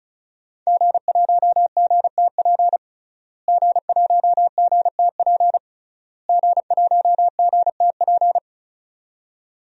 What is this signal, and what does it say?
Morse code, heard as a keyed tone, about 36 wpm, 690 Hz. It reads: G1GTP G1GTP G1GTP
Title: DXCC-100 Call Signs - 5 Characters - Repeated 3x - 35wpm